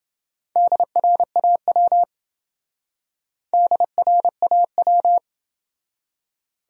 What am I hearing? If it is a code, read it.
Morse code, heard as a keyed tone, about 28 wpm, 700 Hz. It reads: DRAW DRAW